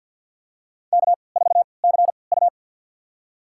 Morse code, heard as a keyed tone, about 47 wpm, spaced as for 28 wpm, 700 Hz. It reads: K4CU
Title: USA Call Signs - 4 Characters - 3x Character Spacing - 50wpm